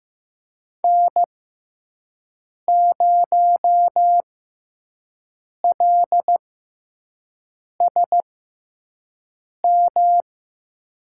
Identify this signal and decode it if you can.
Morse code, heard as a keyed tone, about 15 wpm, spaced as for 5 wpm, 700 Hz. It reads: N0LSM
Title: USA Call Signs - 6x Character Spacing - 15wpm